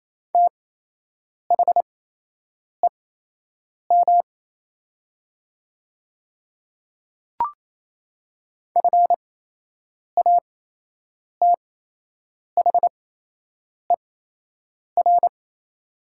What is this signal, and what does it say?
Morse code, heard as a keyed tone, about 28 wpm, spaced as for 8 wpm, 700 Hz. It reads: THEM FATHER
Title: 8x Character Spacing - Top 200 Words - 28wpm